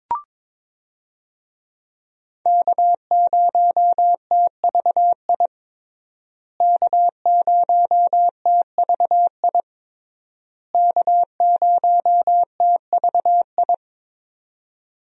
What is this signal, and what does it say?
Morse code, heard as a keyed tone, about 22 wpm, 700 Hz. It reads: K0TVI K0TVI K0TVI
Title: USA Call Signs - 5 Characters - Repeated 3x - 22wpm